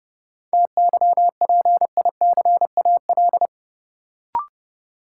Morse code, heard as a keyed tone, about 30 wpm, 700 Hz. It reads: TYPICAL
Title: Sentences from Top 1100 Words - New Word Drill Alt - 30wpm